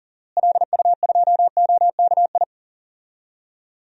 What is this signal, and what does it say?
Morse code, heard as a keyed tone, about 37 wpm, 700 Hz. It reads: LU2OKI
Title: DXCC-100 Call Signs - 6 Characters - Repeated 3x - 40wpm